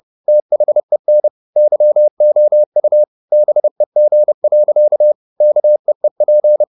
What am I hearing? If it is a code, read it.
Morse code, heard as a keyed tone, about 30 wpm, 600 Hz. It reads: THEN YOU BEG. KEEP
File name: the old man and the sea / the old man and the sea 1 30wpm